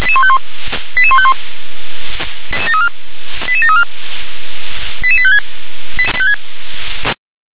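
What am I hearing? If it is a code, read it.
Five-tone selective call (ZVEI), heard as quick runs of five tones, tones of about 70 ms. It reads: 88151 88734 88856
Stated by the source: ZVEI five-tone selective call received with an SDR.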